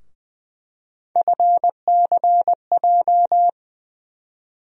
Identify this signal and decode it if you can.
Morse code, heard as a keyed tone, about 20 wpm, 700 Hz. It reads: FCJ
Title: USA Call Sign Postfixes - 3 Letters 20wpm